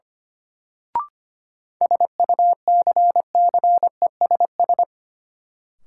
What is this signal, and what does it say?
Morse code, heard as a keyed tone, about 25 wpm, 700 Hz. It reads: SUCCESS
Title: Sentences from Top 900 Words - New Word Drill - 25wpm